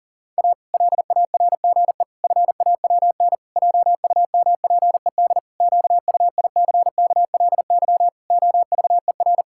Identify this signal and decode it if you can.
Morse code, heard as a keyed tone, about 40 wpm, 700 Hz. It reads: A LARGE FAWN JUMPED QUICKLY OVER
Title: Pangram Challenge v2 - 40wpm